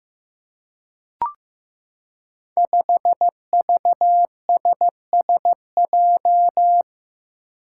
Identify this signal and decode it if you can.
Morse code, heard as a keyed tone, about 15 wpm, 700 Hz. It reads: 5VSSJ
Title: Five Challenging Letter-Number Combinations 15wpm